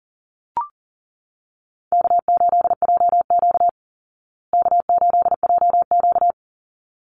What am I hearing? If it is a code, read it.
Morse code, heard as a keyed tone, about 40 wpm, 700 Hz. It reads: K8JQ K8JQ
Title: Easy Call Signs Repeated 2x - 40wpm